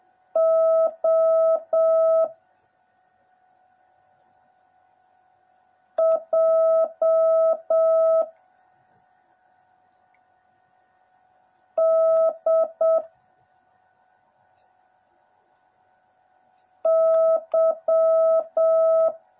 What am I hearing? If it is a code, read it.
Morse code, heard as a keyed tone, about 7 wpm, 650 Hz. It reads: O J D Y
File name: jh8cbh / 2021cw / cw09a